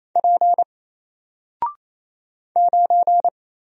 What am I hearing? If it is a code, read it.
Morse code, heard as a keyed tone, about 27 wpm, 700 Hz. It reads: P 9
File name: Single-Letter-Number-28wpm